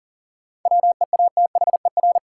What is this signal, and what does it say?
Morse code, heard as a keyed tone, about 40 wpm, 700 Hz. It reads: WEATHER
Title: Sentences from Top 800 Words - New Word Drill - 40wpm